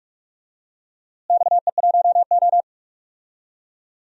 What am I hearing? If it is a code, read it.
Morse code, heard as a keyed tone, about 44 wpm, 700 Hz. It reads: XE1O
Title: DXCC-100 Call Signs - 4 Characters - Repeated 2x - 45wpm